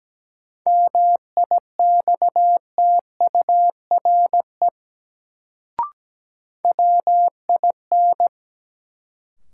Morse code, heard as a keyed tone, about 17 wpm, 700 Hz. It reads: MIXTURE WIN
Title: Sentences from Top 1500 Words - New Word Drill - 17wpm